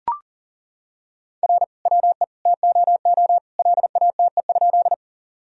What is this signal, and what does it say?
Morse code, heard as a keyed tone, about 40 wpm, 700 Hz. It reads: R WE TOO LATE?
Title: Sentences from Top 300 Words and Abbreviations 40wpm